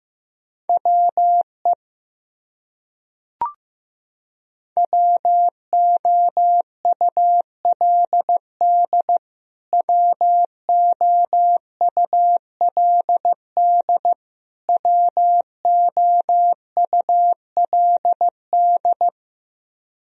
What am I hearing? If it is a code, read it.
Morse code, heard as a keyed tone, about 15 wpm, 700 Hz. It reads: WE WOULD WOULD WOULD
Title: Top 100 Words Repeated 15wpm